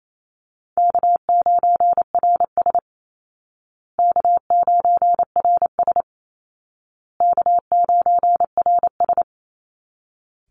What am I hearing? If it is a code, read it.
Morse code, heard as a keyed tone, about 28 wpm, 700 Hz. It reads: K9RS K9RS K9RS
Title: Easy Call Signs Repeated 3x - 28wpm